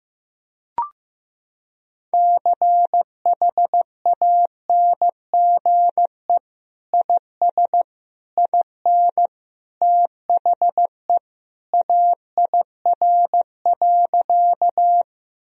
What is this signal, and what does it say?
Morse code, heard as a keyed tone, about 15 wpm, 700 Hz. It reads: CHANGE IS IN THE AIR.
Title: Sentences from Top 1000 Words - Review Part 1 - 15wpm